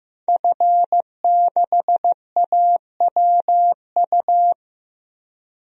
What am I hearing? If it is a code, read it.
Morse code, heard as a keyed tone, about 15 wpm, 700 Hz. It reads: F6AWU
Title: DXCC-100 Call Signs - 5 Characters - 15wpm